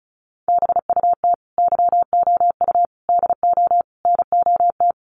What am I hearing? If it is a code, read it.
Morse code, heard as a keyed tone, about 34 wpm, 700 Hz. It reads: BUT YOU DO NOT
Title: Sentences from Top 1100 Words 35wpm